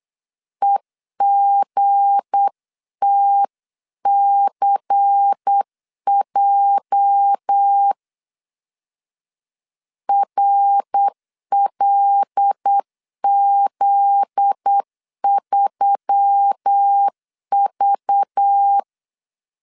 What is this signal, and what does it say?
Morse code, heard as a keyed tone, about 8 wpm, 800 Hz. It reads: EGTCJ RLZ3V
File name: CW / DET / 1 9 1500C S B 8wpm 5624